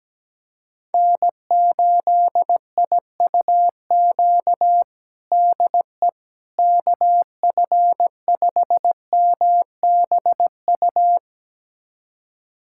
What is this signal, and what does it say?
Morse code, heard as a keyed tone, about 17 wpm, 700 Hz. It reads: N8IUQ DE KF5MBU
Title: USA Callsign DE Callsign 17wpm